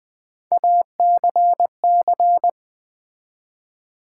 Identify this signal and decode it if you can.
Morse code, heard as a keyed tone, about 20 wpm, 700 Hz. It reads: ACC
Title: Build A Word - 3 Letters At A Time - 9-Letter Words - 20wpm